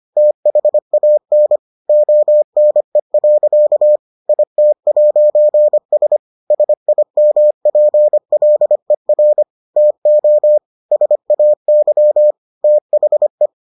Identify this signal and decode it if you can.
Morse code, heard as a keyed tone, about 25 wpm, 600 Hz. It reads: THAN ONE. IT'S SIMPLER TO SAY THE